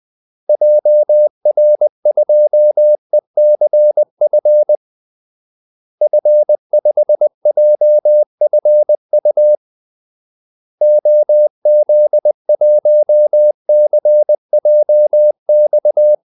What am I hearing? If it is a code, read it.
Morse code, heard as a keyed tone, about 20 wpm, 600 Hz. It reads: JR2ECF F5JFU OZ1CJX
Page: Kaldesignaler 20 wpm